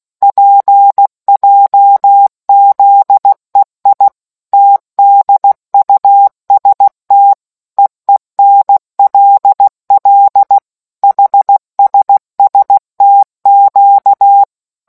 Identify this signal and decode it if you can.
Morse code, heard as a keyed tone, about 16 wpm, 800 Hz. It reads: PJZEI TDUST EENLL HSSTQ